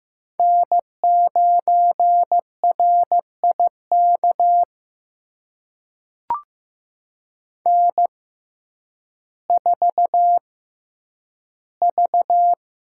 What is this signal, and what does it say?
Morse code, heard as a keyed tone, about 15 wpm, spaced as for 6 wpm, 700 Hz. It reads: N9RI KN4V
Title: USA Call Signs - 6x Character Spacing - 15wpm